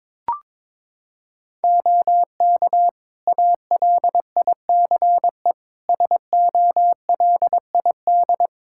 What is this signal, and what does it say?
Morse code, heard as a keyed tone, about 21 wpm, 700 Hz. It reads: OK ALICE SOLID
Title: Practice QSO - Exchange 3 - More Info - 22wpm